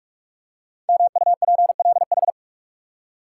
Morse code, heard as a keyed tone, about 44 wpm, 700 Hz. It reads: MUPLH
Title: Code Groups - 5 Letters - 45wpm